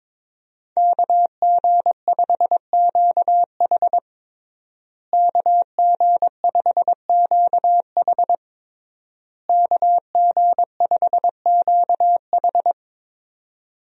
Morse code, heard as a keyed tone, about 21 wpm, 700 Hz. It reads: KG5QH KG5QH KG5QH
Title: USA Call Signs - 5 Characters - Repeated 3x - 22wpm